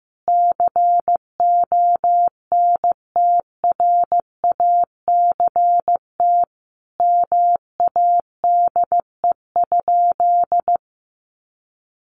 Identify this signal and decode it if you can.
Morse code, heard as a keyed tone, about 15 wpm, 700 Hz. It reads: CONTRACT MADE?
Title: Sentences from Top 1400 Words 15wpm